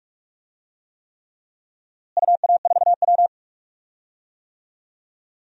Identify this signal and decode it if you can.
Morse code, heard as a keyed tone, about 46 wpm, 700 Hz. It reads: UA4W
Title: DXCC-100 Call Signs - 4 Characters - Repeated 3x - 45wpm